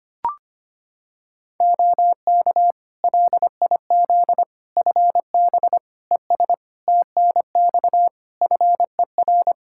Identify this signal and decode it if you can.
Morse code, heard as a keyed tone, about 25 wpm, 700 Hz. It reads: OK LIZ FB ES TNX FER
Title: K1USN SST QSO Practice 25wpm